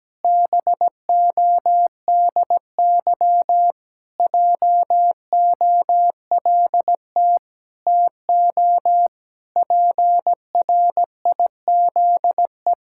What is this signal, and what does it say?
Morse code, heard as a keyed tone, about 17 wpm, 700 Hz. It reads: BODY JOLT TO PRIZE
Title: Pangram Challenge v2 - 17wpm